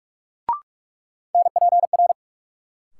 Morse code, heard as a keyed tone, about 46 wpm, 700 Hz. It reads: NPR
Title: Sets of 5 Words - New Word Drill - 45wpm